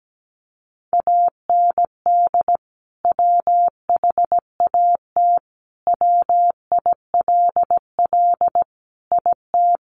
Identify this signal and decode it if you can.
Morse code, heard as a keyed tone, about 17 wpm, 700 Hz. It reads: AND WHAT WILL IT
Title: Sentences from Top 1200 Words 17wpm